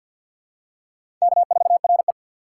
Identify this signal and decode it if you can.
Morse code, heard as a keyed tone, about 52 wpm, 700 Hz. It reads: K4RE
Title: USA Call Signs - 4 Characters - 3x Character Spacing - 50wpm